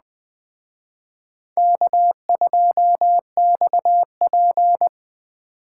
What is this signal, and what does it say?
Morse code, heard as a keyed tone, about 20 wpm, 700 Hz. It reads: K2XP
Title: Speed Racing - USA Call Signs 17wpm